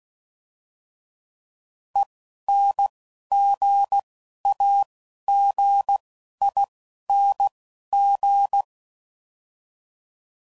Morse code, heard as a keyed tone, about 16 wpm, 800 Hz. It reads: E N G A G I N G